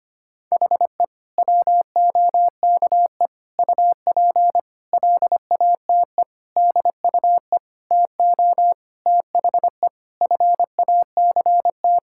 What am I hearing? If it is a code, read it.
Morse code, heard as a keyed tone, about 25 wpm, 700 Hz. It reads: HE WOKE UP LATE DUE TO THE FACT
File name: practice / 25wpm / Sentences with 4-Word Phrases - 25wpm